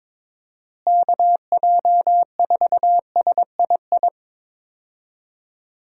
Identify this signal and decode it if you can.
Morse code, heard as a keyed tone, about 22 wpm, 700 Hz. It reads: KJ4SII
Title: USA Call Signs - 6 Characters - Repeated 2x - 22wpm